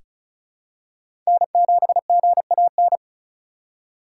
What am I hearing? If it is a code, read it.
Morse code, heard as a keyed tone, about 36 wpm, 700 Hz. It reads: N7GAN
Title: USA Call Signs - 4x Character Spacing - 35wpm